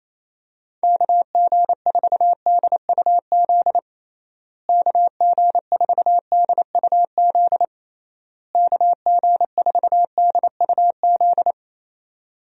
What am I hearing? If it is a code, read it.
Morse code, heard as a keyed tone, about 28 wpm, 700 Hz. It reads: KG4DUZ KG4DUZ KG4DUZ
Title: USA Call Signs - 6 Characters - Repeated 3x - 28wpm